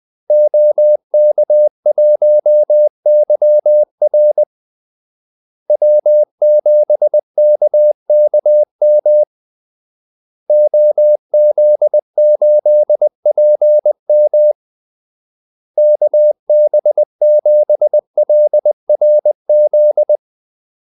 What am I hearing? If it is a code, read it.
Morse code, heard as a keyed tone, about 20 wpm, 600 Hz. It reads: OK1YR W7KKM OZ8PM KB7LRZ
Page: Kaldesignaler 20 wpm